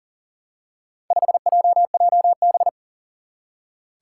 Morse code, heard as a keyed tone, about 43 wpm, 700 Hz. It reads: 5JJB
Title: Four Challenging Letter-Number Combinations 40wpm